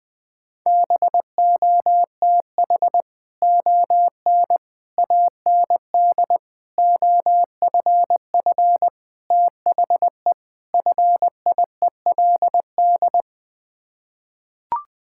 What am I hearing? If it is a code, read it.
Morse code, heard as a keyed tone, about 20 wpm, 700 Hz. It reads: BOTH ON AND OFF THE FIELD
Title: Sets of 6 Words - 20wpm